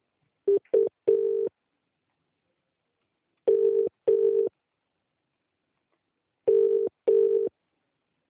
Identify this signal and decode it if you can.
Telephone call progress tone (UK), heard as a telephone network's tone, about 400, 450 Hz. Ringback tone.